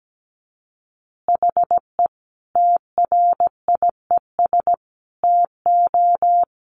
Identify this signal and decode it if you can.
Morse code, heard as a keyed tone, about 17 wpm, 700 Hz. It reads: HE TRIES TO
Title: Easy Phrases 17wpm